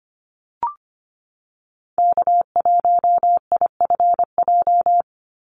Morse code, heard as a keyed tone, about 25 wpm, 700 Hz. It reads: K1IFJ
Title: Call Signs Repeated 3x - 25wpm